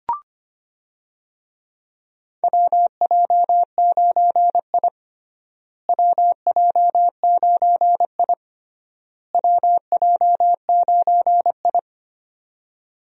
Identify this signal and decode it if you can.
Morse code, heard as a keyed tone, about 25 wpm, 700 Hz. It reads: WJ9I WJ9I WJ9I
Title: USA Call Signs - 4 Characters - Repeated 3x - 25wpm